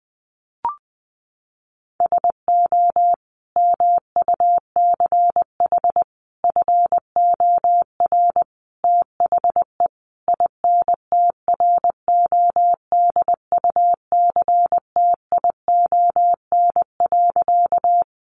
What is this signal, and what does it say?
Morse code, heard as a keyed tone, about 20 wpm, 700 Hz. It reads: SO MUCH FOR THE INTRODUCTION.
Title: Sentences from Top 1000 Words 20wpm